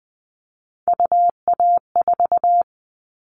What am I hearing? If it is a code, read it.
Morse code, heard as a keyed tone, about 20 wpm, 700 Hz. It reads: UA4
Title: Callsign Prefixes for Top 25 DXCC - Repeated 3x - 20wpm